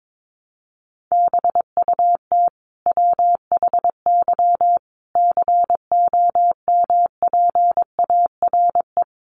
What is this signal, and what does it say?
Morse code, heard as a keyed tone, about 22 wpm, 700 Hz. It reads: BUT WHY COMPARE
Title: Sentences from Top 1300 Words 22wpm